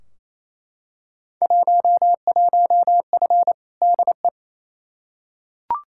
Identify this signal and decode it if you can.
Morse code, heard as a keyed tone, about 28 wpm, 700 Hz. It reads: 11F DE
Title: Field Day Exchanges Encore - 28wpm